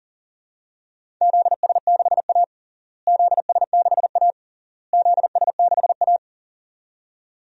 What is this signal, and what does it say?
Morse code, heard as a keyed tone, about 40 wpm, 700 Hz. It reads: ZS6A ZS6A ZS6A
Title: DXCC-100 Call Signs - 4 Characters - Repeated 3x - 40wpm